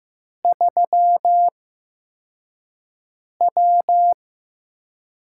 Morse code, heard as a keyed tone, about 15 wpm, spaced as for 4 wpm, 700 Hz. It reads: 3W
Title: USA Call Signs - 6 Characters - 8x Character Spacing - 15wpm